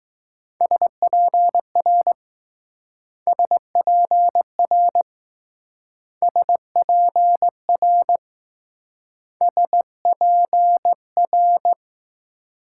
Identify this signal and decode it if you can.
Morse code, heard as a keyed tone, about 20 wpm, 700 Hz. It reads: SPR SPR SPR SPR